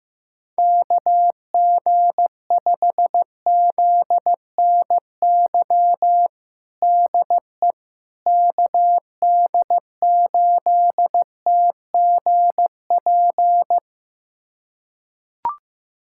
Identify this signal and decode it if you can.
Morse code, heard as a keyed tone, about 15 wpm, 700 Hz. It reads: KG5ZNY DE KD8TGP
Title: USA Callsign DE Callsign - 6 Characters - 15wpm